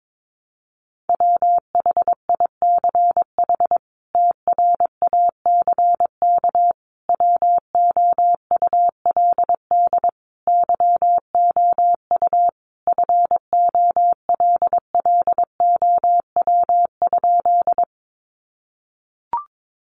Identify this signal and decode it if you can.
Morse code, heard as a keyed tone, about 22 wpm, 700 Hz. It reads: WHICH TRACK WOULD YOU FOLLOW?
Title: Sentences from Top 1100 Words 22wpm